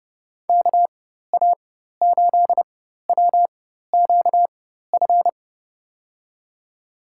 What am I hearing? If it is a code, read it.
Morse code, heard as a keyed tone, about 30 wpm, spaced as for 14 wpm, 700 Hz. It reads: KA8WQF
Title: USA Call Signs - 6 Characters - 4x Character Spacing - 30wpm